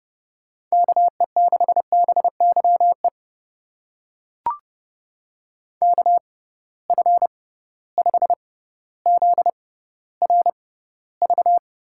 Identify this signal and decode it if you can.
Morse code, heard as a keyed tone, about 30 wpm, spaced as for 15 wpm, 700 Hz. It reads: KE6BYE KF5ZRV
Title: USA Call Signs - 6 Characters - 6x Character Spacing - 30wpm